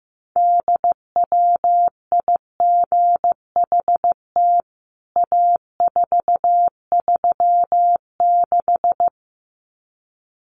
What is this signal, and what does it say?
Morse code, heard as a keyed tone, about 15 wpm, 700 Hz. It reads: DWIGHT A436
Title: CWT Exchange 15wpm